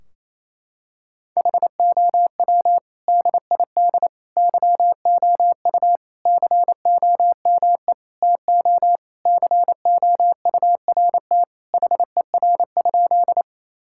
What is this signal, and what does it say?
Morse code, heard as a keyed tone, about 28 wpm, 700 Hz. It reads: HOW DID YOU COME TO COURT HER?
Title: Sentences from Top 1600 Words 28wpm